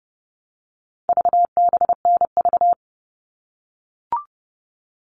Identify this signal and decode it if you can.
Morse code, heard as a keyed tone, about 30 wpm, 700 Hz. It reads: VBNV
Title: All Cut Numbers - 4-Digit 30wpm